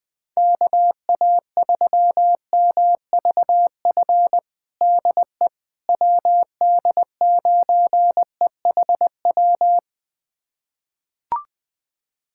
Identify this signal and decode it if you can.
Morse code, heard as a keyed tone, about 21 wpm, 700 Hz. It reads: KA3MVF DE WD9EHW
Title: USA Callsign DE Callsign 20wpm